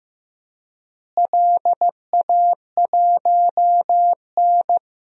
Morse code, heard as a keyed tone, about 15 wpm, 700 Hz. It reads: LA1N
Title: DXCC-100 Call Signs - 4 Characters - Repeated 2x - 15wpm